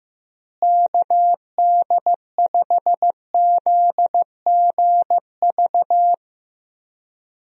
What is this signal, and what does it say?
Morse code, heard as a keyed tone, about 15 wpm, 700 Hz. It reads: KD5ZGV
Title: USA Call Signs - 6 Characters - Repeated 2x - 15wpm